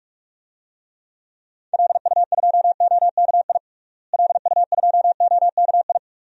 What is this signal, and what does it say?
Morse code, heard as a keyed tone, about 46 wpm, 700 Hz. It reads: LU2OKI LU2OKI
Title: DXCC-100 Call Signs - 6 Characters - Repeated 3x - 45wpm